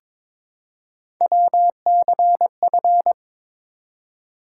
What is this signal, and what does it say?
Morse code, heard as a keyed tone, about 22 wpm, 700 Hz. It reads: WCF
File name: ARRL Sections v2 - 22wpm